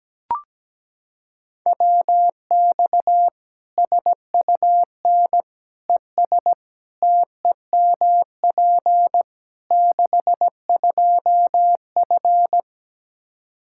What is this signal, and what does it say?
Morse code, heard as a keyed tone, about 17 wpm, 700 Hz. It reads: WX SUN ES TEMP 62F
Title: Practice QSO - Exchange 2 - Info - 17wpm